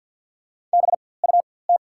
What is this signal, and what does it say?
Morse code, heard as a keyed tone, about 48 wpm, spaced as for 23 wpm, 690 Hz. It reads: BUT
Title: 4x Character Spacing - Top 300 Words - 50wpm